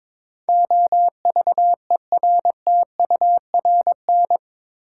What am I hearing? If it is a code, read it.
Morse code, heard as a keyed tone, about 21 wpm, 700 Hz. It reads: OVERTURN
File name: practice / 22wpm / Eight Letter Words - 22wpm